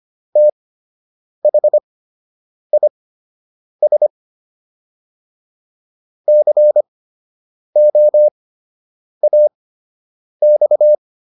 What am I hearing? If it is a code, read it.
Morse code, heard as a keyed tone, about 25 wpm, spaced as for 8 wpm, 600 Hz. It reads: THIS COAX